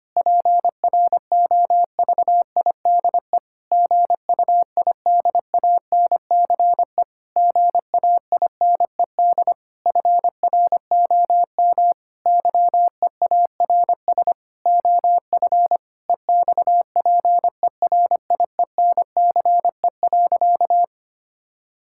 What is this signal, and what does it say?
Morse code, heard as a keyed tone, about 25 wpm, 700 Hz. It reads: PROVIDE GUIDANCE GAINED FROM YEARS OF EXPERIENCE.
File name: practice / 25wpm / Difficult Sentences with Hints - 25wpm